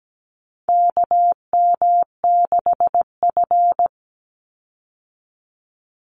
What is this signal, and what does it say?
Morse code, heard as a keyed tone, about 17 wpm, 700 Hz. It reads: KM6F
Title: USA Call Signs 17wpm